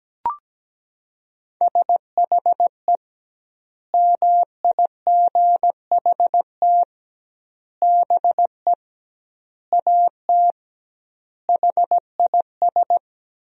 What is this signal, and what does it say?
Morse code, heard as a keyed tone, about 17 wpm, 700 Hz. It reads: SHE MIGHT BE AT HIS
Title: 2x Word-Spacing - Sentences from Top 300 Words - 17wpm